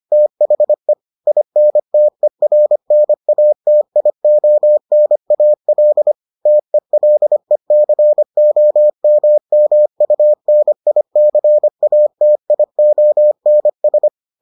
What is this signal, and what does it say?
Morse code, heard as a keyed tone, about 25 wpm, 600 Hz. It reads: THE INTERNATIONAL TELECOMMUNICATIONS